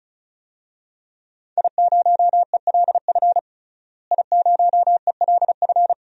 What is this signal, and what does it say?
Morse code, heard as a keyed tone, about 36 wpm, 700 Hz. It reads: I0ELF I0ELF
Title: DXCC-100 Call Signs - 5 Characters - Repeated 2x - 35wpm